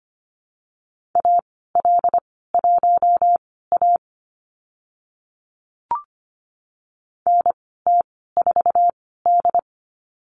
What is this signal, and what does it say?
Morse code, heard as a keyed tone, about 25 wpm, spaced as for 16 wpm, 700 Hz. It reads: AL1A NT4D
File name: Lesson 15g - Number 5 - Call Signs 25wpm